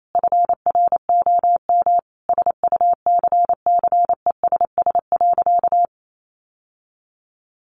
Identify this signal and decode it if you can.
Morse code, heard as a keyed tone, about 28 wpm, 700 Hz. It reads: FROM SUCCESS.
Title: Sentences from Top 900 Words 28wpm